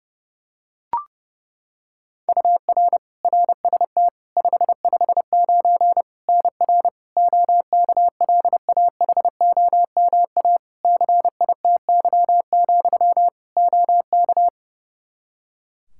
Morse code, heard as a keyed tone, about 32 wpm, 700 Hz. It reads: UR RST 559 NR OKLAHOMA CITY, OK
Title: Practice QSO - Exchange 1 - Beginning - 30wpm